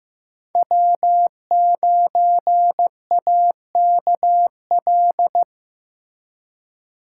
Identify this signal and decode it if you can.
Morse code, heard as a keyed tone, about 15 wpm, 700 Hz. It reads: W9AKL
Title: USA Call Signs - 5 Characters - 4x Character Spacing - 15wpm